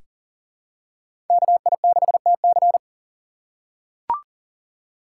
Morse code, heard as a keyed tone, about 40 wpm, 700 Hz. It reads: KI6TC
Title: USA Call Signs - 5 Characters - 3x Character Spacing - 40wpm